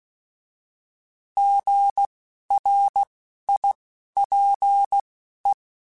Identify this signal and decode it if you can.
Morse code, heard as a keyed tone, about 16 wpm, spaced as for 11 wpm, 800 Hz. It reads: GRIPE
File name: Alphabetical / Alphabetical mp3s / Gs @10